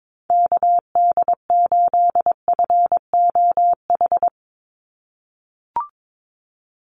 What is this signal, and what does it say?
Morse code, heard as a keyed tone, about 22 wpm, 700 Hz. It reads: KD8FOH
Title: USA Call Signs 22wpm